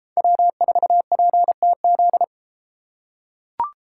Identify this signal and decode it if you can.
Morse code, heard as a keyed tone, about 30 wpm, 700 Hz. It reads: W4PTZ